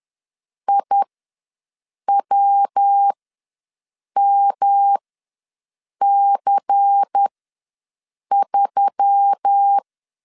Morse code, heard as a keyed tone, about 10 wpm, spaced as for 6 wpm, 800 Hz. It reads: IWMC3
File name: LEZIONI / 7 GF9Z / 7 GF9 ws 10wpm 4034